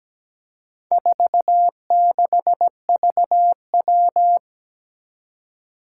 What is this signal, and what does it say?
Morse code, heard as a keyed tone, about 17 wpm, 700 Hz. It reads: 46VW
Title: Four Challenging Letter-Number Combinations 17wpm